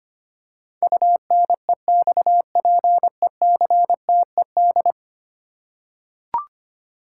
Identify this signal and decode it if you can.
Morse code, heard as a keyed tone, about 25 wpm, 700 Hz. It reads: UNEXPECTED
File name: Ten Letter Words - 25wpm